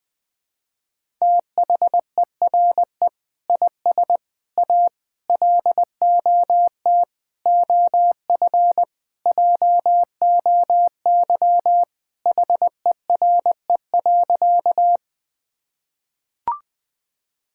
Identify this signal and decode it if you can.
Morse code, heard as a keyed tone, about 20 wpm, 700 Hz. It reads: THERE IS A LOT OF JOY HERE.
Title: Sentences with 5-Word Phrases - 20wpm